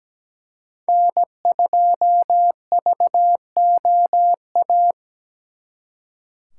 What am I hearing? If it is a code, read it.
Morse code, heard as a keyed tone, about 20 wpm, 700 Hz. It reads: N2VOA
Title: Speed Racing - USA 1x3 Call Signs 17wpm